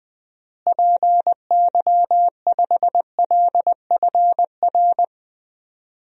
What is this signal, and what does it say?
Morse code, heard as a keyed tone, about 20 wpm, 700 Hz. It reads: PY5LFR
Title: DXCC-100 Call Signs - 6 Characters - Repeated 2x - 20wpm